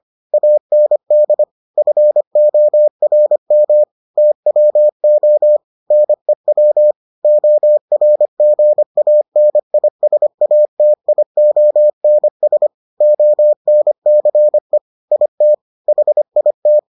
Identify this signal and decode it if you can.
Morse code, heard as a keyed tone, about 25 wpm, 600 Hz. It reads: AND FORM TWO NEW ORGANISATIONS ONCE IT HIT